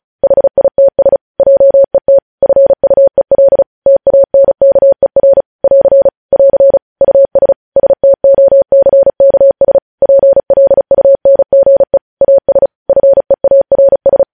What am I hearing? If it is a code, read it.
Morse code, heard as a keyed tone, about 36 wpm, 550 Hz. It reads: HITS JET FUEL TANKER <AR> <AR> US STOCKS PLUNGE AS FEARS
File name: podcast / News headlines-1x-35x35